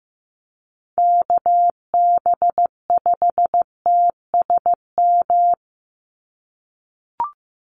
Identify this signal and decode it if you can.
Morse code, heard as a keyed tone, about 15 wpm, 700 Hz. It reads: KB5TSM
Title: USA Call Signs 15wpm